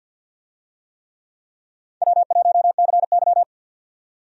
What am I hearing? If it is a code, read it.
Morse code, heard as a keyed tone, about 47 wpm, 700 Hz. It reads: W1CY
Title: USA Call Signs - 4 Characters - 50wpm